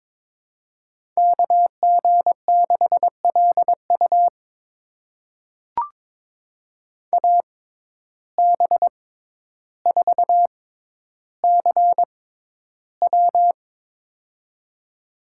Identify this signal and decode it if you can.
Morse code, heard as a keyed tone, about 21 wpm, spaced as for 11 wpm, 700 Hz. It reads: KG6LU AB4CW
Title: USA Call Signs - 5 Characters - 6x Character Spacing - 22wpm